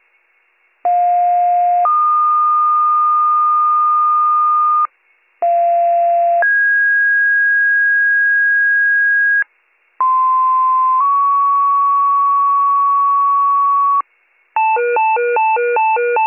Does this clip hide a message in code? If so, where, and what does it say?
Yes: two-tone page from 0.8 to 14.0 s, heard as a one-second tone, then a three-second tone, about 710 then 1220 Hz, about 690 then 1650 Hz, about 1040 then 1120 Hz.